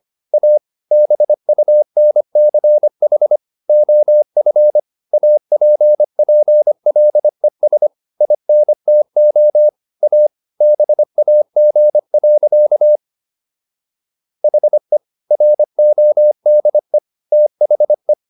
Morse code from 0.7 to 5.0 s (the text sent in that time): BUNCH OF